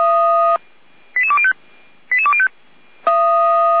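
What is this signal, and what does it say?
Five-tone selective call (ZVEI), heard as quick runs of five tones, tones of about 70 ms. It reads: 88205 88205
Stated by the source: ZVEI five-tone selective call received via SDR.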